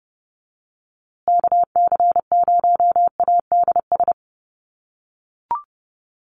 Morse code, heard as a keyed tone, about 30 wpm, 700 Hz. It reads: KC0ADS
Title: USA Call Signs Repeated 2x 30wpm